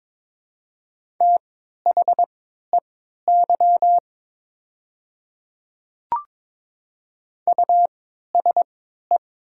Morse code, heard as a keyed tone, about 21 wpm, spaced as for 12 wpm, 700 Hz. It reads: THEY USE
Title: 3x Character Spacing - Top 100 Words - 22wpm